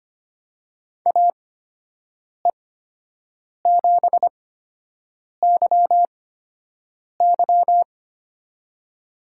Morse code, heard as a keyed tone, about 25 wpm, spaced as for 7 wpm, 700 Hz. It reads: AE7YY
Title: USA Call Signs - 5 Characters - 8x Character Spacing - 25wpm